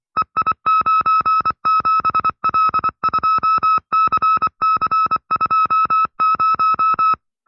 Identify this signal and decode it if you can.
Morse code, heard as a keyed tone, about 24 wpm, 1310 Hz. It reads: EI97L2CC20
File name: FreeFiles / CW-practice / 20WPM-side2